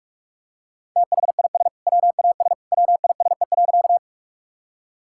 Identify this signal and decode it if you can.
Morse code, heard as a keyed tone, about 46 wpm, 700 Hz. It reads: THIS WAS WISE.
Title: Sentences from Top 1600 Words 45wpm